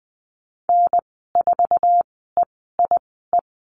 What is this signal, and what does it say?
Morse code, heard as a keyed tone, about 20 wpm, spaced as for 15 wpm, 700 Hz. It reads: N4EIE
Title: 055 Lesson 10g - Letter r - Call Signs 20wpm